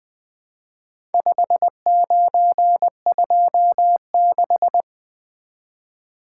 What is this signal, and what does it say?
Morse code, heard as a keyed tone, about 20 wpm, 700 Hz. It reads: 5926
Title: Four Letter-Number Combinations - 20wpm